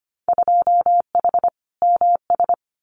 Morse code, heard as a keyed tone, about 25 wpm, 700 Hz. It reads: 2H MS
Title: Winter Field Day Exchanges 25wpm